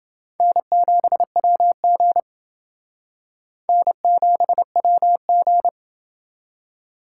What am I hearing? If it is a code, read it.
Morse code, heard as a keyed tone, about 28 wpm, 700 Hz. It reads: N7WG N7WG